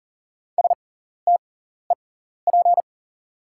Morse code, heard as a keyed tone, about 40 wpm, spaced as for 14 wpm, 700 Hz. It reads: STEP
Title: 6x Character Spacing - Top 500 Words - 40wpm